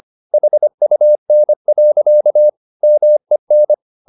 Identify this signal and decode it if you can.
Morse code, heard as a keyed tone, about 25 wpm, 600 Hz. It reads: HUN. MEN
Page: Ferie 25 wpm